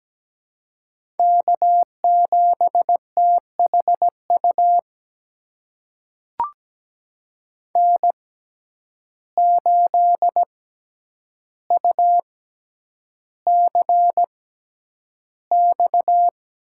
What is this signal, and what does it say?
Morse code, heard as a keyed tone, about 17 wpm, spaced as for 9 wpm, 700 Hz. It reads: K7THU N8UCX
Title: USA Call Signs - 5 Characters - 6x Character Spacing - 17wpm